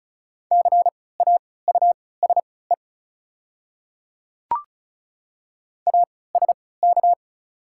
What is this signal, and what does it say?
Morse code, heard as a keyed tone, about 36 wpm, spaced as for 20 wpm, 700 Hz. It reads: CAUSE ASK
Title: 3x Character Spacing - Top 500 Words - 35wpm